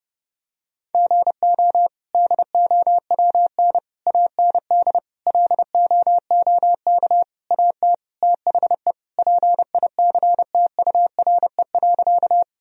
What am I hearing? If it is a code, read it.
Morse code, heard as a keyed tone, about 30 wpm, 700 Hz. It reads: GO DOWN AND LOOK AT THE PICTURE.
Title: Sentences from Top 1000 Words - Review Part 1 - 30wpm